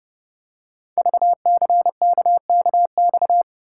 Morse code, heard as a keyed tone, about 30 wpm, 700 Hz. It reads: VCKKX
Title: Code Groups - 5 Letters - 30wpm